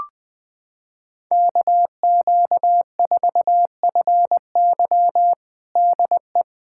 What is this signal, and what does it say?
Morse code, heard as a keyed tone, about 20 wpm, 700 Hz. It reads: KQ4FY DE
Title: USA Callsign DE Callsign 20wpm